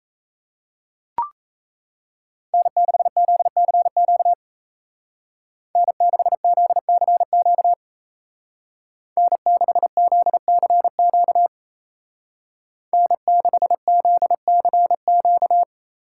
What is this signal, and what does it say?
Morse code, heard as a keyed tone, about 34 wpm, 700 Hz. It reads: N6ZCQ N6ZCQ N6ZCQ N6ZCQ
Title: Speed Racing - USA Call Signs 28wpm